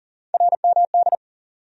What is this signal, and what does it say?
Morse code, heard as a keyed tone, about 40 wpm, 700 Hz. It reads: RMD
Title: USA Call Sign Postfixes - 3 Letters 40wpm